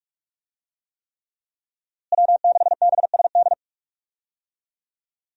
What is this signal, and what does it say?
Morse code, heard as a keyed tone, about 41 wpm, 700 Hz. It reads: W6BSD
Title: USA Call Signs - 5 Characters - Repeated 3x - 45wpm